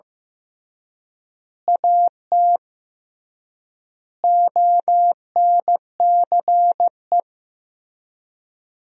Morse code, heard as a keyed tone, about 15 wpm, 700 Hz. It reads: AT ONCE
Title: 3x Word-Spacing - Sentences from Top 300 Words - 15wpm